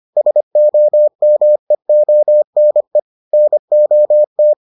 Morse code, heard as a keyed tone, about 25 wpm, 600 Hz. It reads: SOMEONE NOT